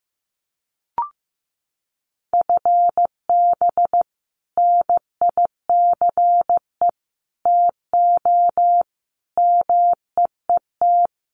Morse code, heard as a keyed tone, about 15 wpm, 700 Hz. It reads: FB NICE TO MEET
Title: Common QSO Phrases 15wpm